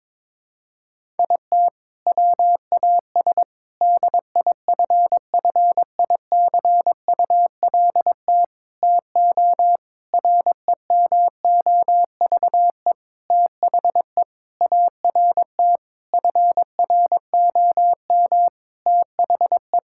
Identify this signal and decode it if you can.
Morse code, heard as a keyed tone, about 22 wpm, 700 Hz. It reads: IT WAS DIFFICULT TO REMOVE THE ART FROM THE
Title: Sentences from Top 1000 Words - Review Part 6 - 22wpm